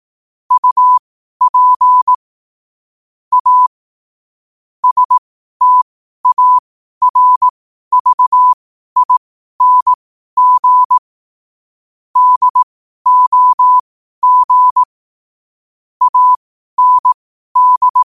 Morse code, heard as a keyed tone, about 18 wpm, spaced as for 12 wpm, 1000 Hz. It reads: UP A STARVING DOG AND
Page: 12 wpm morse code quotes